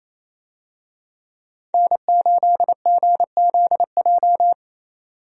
Morse code, heard as a keyed tone, about 28 wpm, 700 Hz. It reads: N8GZJ
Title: USA Call Signs - 5 Characters - 28wpm